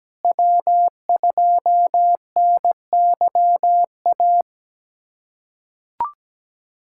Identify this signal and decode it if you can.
Morse code, heard as a keyed tone, about 17 wpm, 700 Hz. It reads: W2NYA
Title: USA Callsign DE Callsign - 5 Characters - 17wpm